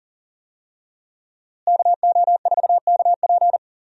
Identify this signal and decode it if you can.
Morse code, heard as a keyed tone, about 39 wpm, 700 Hz. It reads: KO4KP
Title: USA Call Signs - 5 Characters - Repeated 2x - 40wpm